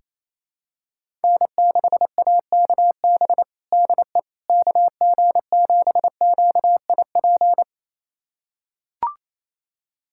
Morse code, heard as a keyed tone, about 28 wpm, 700 Hz. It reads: N6AKB DE KG7QIP
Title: USA Callsign DE Callsign 28wpm